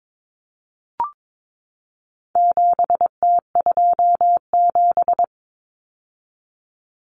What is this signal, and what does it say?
Morse code, heard as a keyed tone, about 22 wpm, 700 Hz. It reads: 7T27
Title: Common Cut Numbers - 4-Digit 22wpm